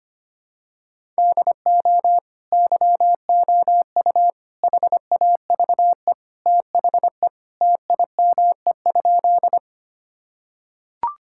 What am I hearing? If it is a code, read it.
Morse code, heard as a keyed tone, about 25 wpm, 700 Hz. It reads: DO YOU HAVE THE TIME?
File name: practice / 25wpm / Sentences with 2-Word Phrases - 25wpm